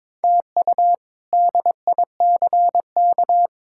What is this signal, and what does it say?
Morse code, heard as a keyed tone, about 22 wpm, 700 Hz. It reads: TU DICK
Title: K1USN SST QSO Practice - 22wpm